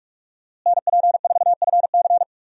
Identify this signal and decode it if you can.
Morse code, heard as a keyed tone, about 46 wpm, 700 Hz. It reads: NP4FC
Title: USA Call Signs - 5 Characters - Repeated 3x - 45wpm